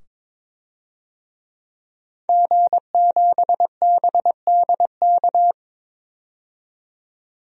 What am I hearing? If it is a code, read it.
Morse code, heard as a keyed tone, about 22 wpm, 700 Hz. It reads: G7BDK
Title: DXCC-100 Call Signs - 5 Characters - 22wpm